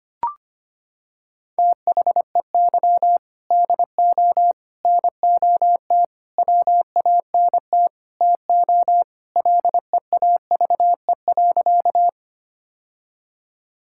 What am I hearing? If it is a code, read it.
Morse code, heard as a keyed tone, about 25 wpm, 700 Hz. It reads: THEY DO NOT WANT TO LEAVE.
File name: practice / 25wpm / Sentences with 3-Word Phrases - 25wpm